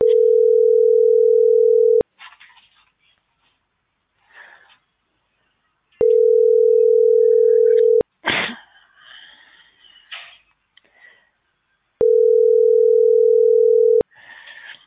Ringback tone (North America), 440 + 480 Hz.